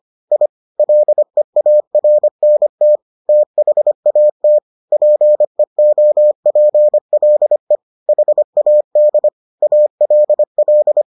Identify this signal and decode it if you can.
Morse code, heard as a keyed tone, about 25 wpm, 600 Hz. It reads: I LEARNT THAT PEOPLE HAD ALL